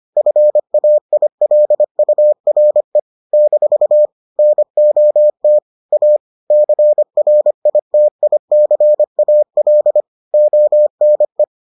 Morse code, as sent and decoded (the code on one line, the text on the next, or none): ..-. .- .. .-.. ..- .-. / -....- / -. --- - / .- / -.-. .-. .. - .. -.-. .- .-.. / --- -. .
FAILURE - NOT A CRITICAL ONE